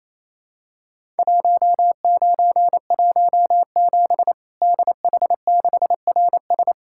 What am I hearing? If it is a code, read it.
Morse code, heard as a keyed tone, about 27 wpm, 700 Hz. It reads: 1917 DH6RS
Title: Worked All Europe DX Contest - QTC Exchange - 28wpm